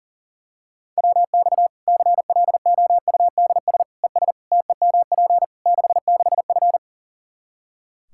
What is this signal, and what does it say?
Morse code, heard as a keyed tone, about 37 wpm, 700 Hz. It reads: WX CLOUDS ES TEMP 66F
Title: Practice QSO - Exchange 2 - Info - 40wpm